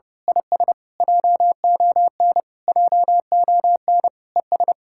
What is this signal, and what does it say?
Morse code, heard as a keyed tone, about 28 wpm, 700 Hz. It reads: IS JON JON ES
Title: Speed-Racing Common QSO Phrases Encore - 30wpm